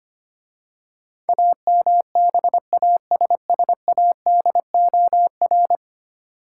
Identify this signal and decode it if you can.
Morse code, heard as a keyed tone, about 25 wpm, 690 Hz. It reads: AMBASSADOR
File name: 10 Letter Words - First 3 Letters Given - 25wpm